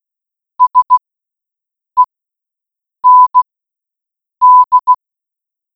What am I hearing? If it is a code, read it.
Morse code, heard as a keyed tone, about 16 wpm, spaced as for 7 wpm, 1000 Hz. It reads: SEND